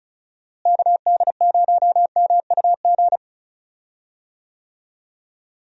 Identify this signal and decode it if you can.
Morse code, heard as a keyed tone, about 36 wpm, 700 Hz. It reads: KD0MUG
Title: USA Call Signs - 6 Characters - Repeated 3x - 35wpm